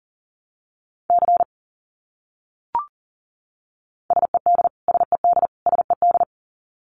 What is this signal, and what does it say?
Morse code, heard as a keyed tone, about 40 wpm, 700 Hz. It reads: C SED SED SED
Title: Abbreviations Repeated 3x - 40wpm